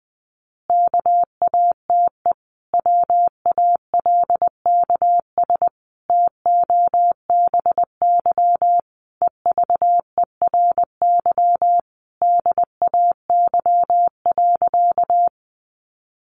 Morse code, heard as a keyed tone, about 20 wpm, 700 Hz. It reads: KATE WALKS TOBY EVERY DAY.
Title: Easy Phrases 20wpm